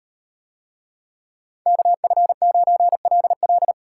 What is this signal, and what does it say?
Morse code, heard as a keyed tone, about 36 wpm, 700 Hz. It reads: KF9LL